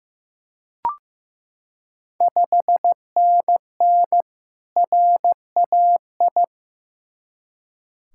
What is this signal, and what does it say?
Morse code, heard as a keyed tone, about 15 wpm, 700 Hz. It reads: 5NN RAI
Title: Ontario QSO Party Exchange 15wpm